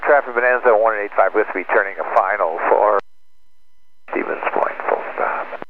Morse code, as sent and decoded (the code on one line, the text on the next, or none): none